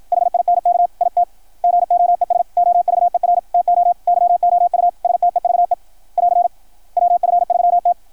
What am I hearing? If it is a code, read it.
Morse code, heard as a keyed tone, about 53 wpm, 700 Hz. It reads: HINK IT GOES OVER TO YOU STEVE <AR> WF3T
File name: archived / CRSnet